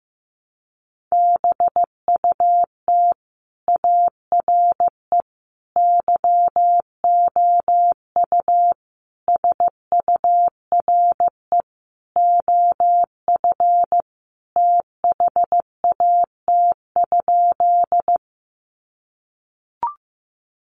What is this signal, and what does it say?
Morse code, heard as a keyed tone, about 15 wpm, 700 Hz. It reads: BUT ARE YOU SURE OF THAT?
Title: Sentences from Top 300 Words 15wpm